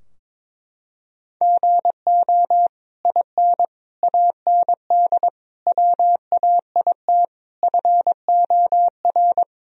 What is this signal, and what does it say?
Morse code, heard as a keyed tone, about 22 wpm, 700 Hz. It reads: GO IN AND WAIT FOR
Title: Sentences from Top 1000 Words - Review Part 2 - 22wpm